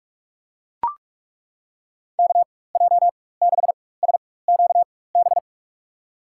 Morse code, heard as a keyed tone, about 43 wpm, spaced as for 21 wpm, 700 Hz. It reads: KJ6SQB
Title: USA Call Signs - 6 Characters - 4x Character Spacing - 45wpm